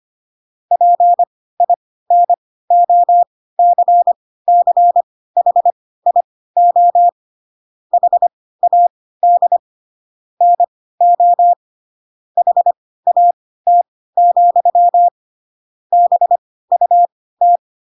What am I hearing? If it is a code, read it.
Morse code, heard as a keyed tone, about 25 wpm, spaced as for 16 wpm, 700 Hz. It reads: PINOCCHIO HAD NO HAT, BUT
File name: Pinocchio / Pinocchio ch6 10 25 ik1hge